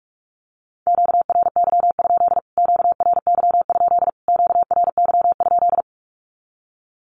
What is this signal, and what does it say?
Morse code, heard as a keyed tone, about 46 wpm, 700 Hz. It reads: QRY? QRY? QRY?
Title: Q-Codes Repeated 3x 45wpm